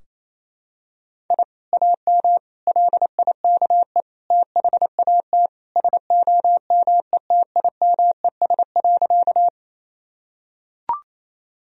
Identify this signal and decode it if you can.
Morse code, heard as a keyed tone, about 28 wpm, 700 Hz. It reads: I AM LIKE THAT SOMETIMES.
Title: Sentences from Top 1000 Words - Review Part 3 - 28wpm